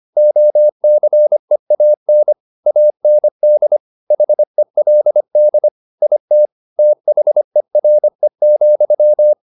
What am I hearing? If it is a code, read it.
Morse code, heard as a keyed tone, about 25 wpm, 600 Hz. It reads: OCEAN AND HELD IT THERE,
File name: the old man and the sea part 1 25wpm